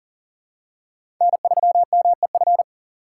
Morse code, heard as a keyed tone, about 37 wpm, 700 Hz. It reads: N3MEF
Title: USA Call Signs - 5 Characters - 6x Character Spacing - 40wpm